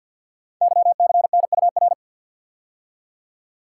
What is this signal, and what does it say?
Morse code, heard as a keyed tone, about 50 wpm, 700 Hz. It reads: YCNUR
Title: Code Groups - 5 Letter:Numbers - 50wpm